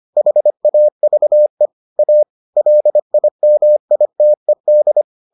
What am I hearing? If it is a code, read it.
Morse code, heard as a keyed tone, about 25 wpm, 600 Hz. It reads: HAVE A LIMITED